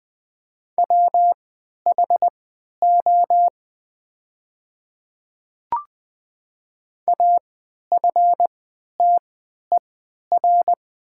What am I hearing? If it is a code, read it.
Morse code, heard as a keyed tone, about 20 wpm, spaced as for 11 wpm, 690 Hz. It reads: WHO AFTER
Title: 3x Character Spacing - Top 200 Words - 20wpm